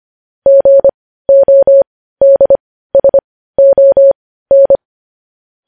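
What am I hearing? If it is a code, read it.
Morse code, heard as a keyed tone, about 25 wpm, spaced as for 15 wpm, 550 Hz. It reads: GODSON
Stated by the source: Morse news headlines sent at 15 wpm.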